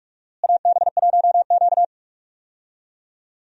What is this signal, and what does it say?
Morse code, heard as a keyed tone, about 47 wpm, 700 Hz. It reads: AB1Q
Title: USA Call Signs - 4 Characters - Repeated 3x - 45wpm